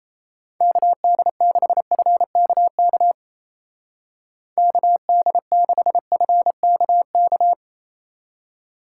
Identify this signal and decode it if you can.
Morse code, heard as a keyed tone, about 35 wpm, 700 Hz. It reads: KD6FKK KD6FKK
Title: Speed Racing - USA 2x3 Call Signs 28wpm